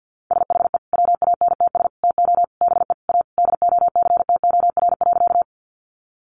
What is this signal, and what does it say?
Morse code, heard as a keyed tone, about 50 wpm, 700 Hz. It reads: SHE WANTS TO BE A DOCTOR.
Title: Easy Phrases 50wpm